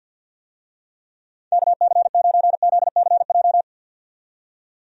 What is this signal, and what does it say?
Morse code, heard as a keyed tone, about 47 wpm, 700 Hz. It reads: KC9ZCJ
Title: USA Call Signs - 6 Characters - 50wpm